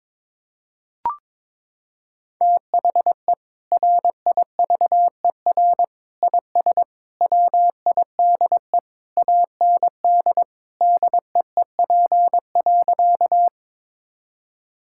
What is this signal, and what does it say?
Morse code, heard as a keyed tone, about 22 wpm, 700 Hz. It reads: THE RIVER IS WIDE AND DEEP.
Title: Sentences from Top 1000 Words - Review Part 6 - 22wpm